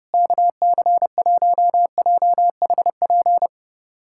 Morse code, heard as a keyed tone, about 30 wpm, 700 Hz. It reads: KC1JHP
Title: USA Callsign DE Callsign 30wpm